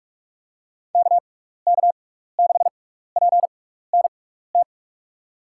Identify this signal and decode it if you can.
Morse code, heard as a keyed tone, about 45 wpm, spaced as for 15 wpm, 700 Hz. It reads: KK6PNT
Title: USA Call Signs - 6 Characters - 6x Character Spacing - 45wpm